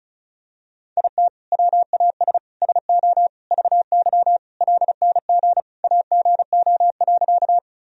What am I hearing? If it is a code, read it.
Morse code, heard as a keyed tone, about 36 wpm, 700 Hz. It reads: IT WAS SO VY LNG AGO.
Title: Sentences from Top 500 Words and Abbreviations 35wpm